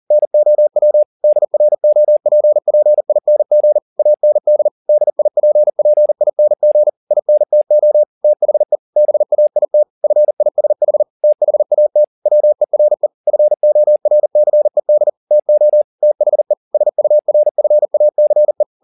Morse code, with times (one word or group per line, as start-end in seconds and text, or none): 0.1-1.0 NOW
1.2-3.8 DROPPING
4.0-4.7 AND
4.9-6.9 DIPPING
7.1-8.0 INTO
8.2-8.8 THE
9.0-9.8 BAIT
10.0-11.0 FISH
11.2-12.0 THAT
12.3-13.1 WERE
13.3-15.1 FORCED
15.3-15.8 TO
16.0-16.5 THE
16.7-18.6 SURFACE